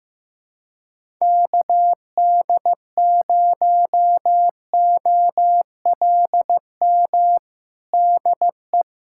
Morse code, heard as a keyed tone, about 16 wpm, 700 Hz. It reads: KD0OLM DE
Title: USA Callsign DE Callsign - 6 Characters - 15wpm